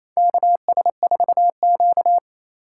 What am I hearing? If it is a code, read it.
Morse code, heard as a keyed tone, about 28 wpm, 700 Hz. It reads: KS4Q
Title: USA Call Signs - 3x Character Spacing - 28wpm